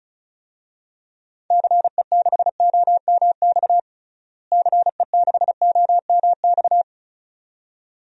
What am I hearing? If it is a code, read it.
Morse code, heard as a keyed tone, about 34 wpm, 700 Hz. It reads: CE6OMX CE6OMX
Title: DXCC-100 Call Signs - 6 Characters - Repeated 2x - 35wpm